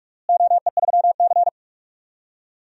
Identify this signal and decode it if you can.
Morse code, heard as a keyed tone, about 45 wpm, 700 Hz. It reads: OE3C
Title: DXCC-100 Call Signs - 4 Characters - Repeated 2x - 45wpm